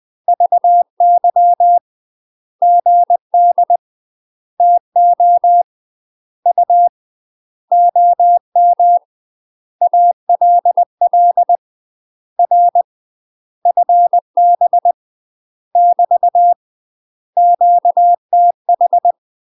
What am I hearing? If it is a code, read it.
Morse code, heard as a keyed tone, about 20 wpm, 700 Hz. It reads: VY GD TO U OM ALL R FB <BT> QTH
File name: UK MASTER QSOs TRAINING / 20WPM / UK MASTER QSOs 0073